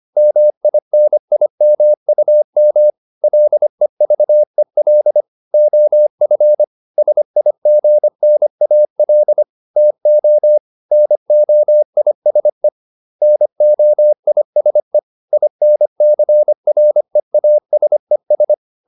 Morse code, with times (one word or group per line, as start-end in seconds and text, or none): 0.2-2.9 MINIMUM
3.2-5.2 LEVEL
5.5-6.6 OF
7.0-9.4 SIGNAL
9.8-10.6 TO
10.9-12.7 NOISE
13.2-15.0 NOISE
15.3-18.5 INCREASES